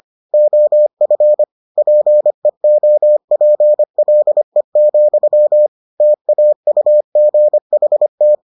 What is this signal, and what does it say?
Morse code, heard as a keyed tone, about 25 wpm, 600 Hz. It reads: OF PEOPLE, TAUGHT